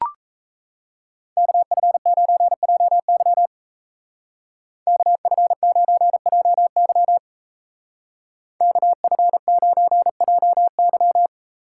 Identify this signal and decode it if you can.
Morse code, heard as a keyed tone, about 33 wpm, 700 Hz. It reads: KF9JY KF9JY KF9JY
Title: Speed Racing - USA 2x2 Call Signs 28wpm